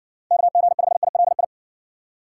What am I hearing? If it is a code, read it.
Morse code, heard as a keyed tone, about 54 wpm, 700 Hz. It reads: KG5ILI